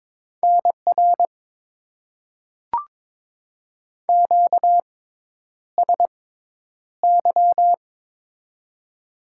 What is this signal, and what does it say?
Morse code, heard as a keyed tone, about 22 wpm, spaced as for 10 wpm, 700 Hz. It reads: NR QSY
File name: Top 90 QSO Elements - 6x Character Spacing - 22wpm